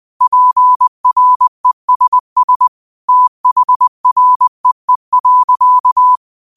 Morse code, heard as a keyed tone, about 20 wpm, 1000 Hz. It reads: PRESS THREE.